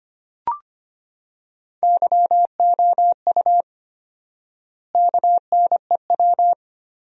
Morse code, heard as a keyed tone, about 25 wpm, 700 Hz. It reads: YOU KNEW
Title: 4x Word-Spacing - Sentences from Top 500 Words - 25wpm